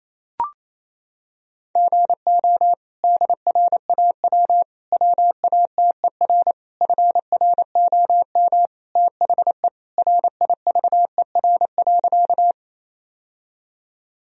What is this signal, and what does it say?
Morse code, heard as a keyed tone, about 28 wpm, 700 Hz. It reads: GO DRAW WATER FROM THE RIVER.
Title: Sentences from Top 1000 Words - Review Part 1 - 28wpm